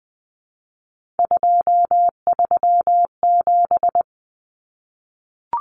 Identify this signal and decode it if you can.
Morse code, heard as a keyed tone, about 20 wpm, 700 Hz. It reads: 237
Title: Numbers - 3-Digit 20wpm